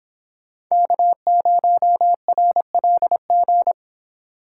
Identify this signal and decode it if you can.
Morse code, heard as a keyed tone, about 26 wpm, 700 Hz. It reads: K0RLG